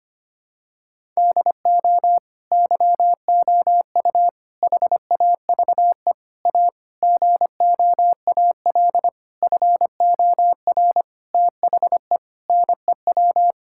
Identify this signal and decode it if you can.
Morse code, heard as a keyed tone, about 25 wpm, 700 Hz. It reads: DO YOU HAVE A GOAL FOR THE NEW
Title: Sentences from Top 1000 Words - Review Part 5 - 25wpm